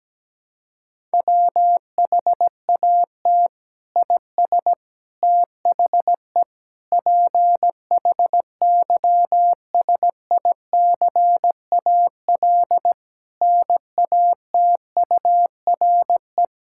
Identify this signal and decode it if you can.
Morse code, heard as a keyed tone, about 17 wpm, 700 Hz. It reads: WHAT IS THE PHYSICAL NATURE
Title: Sentences from Top 1000 Words - Review Part 6 - 17wpm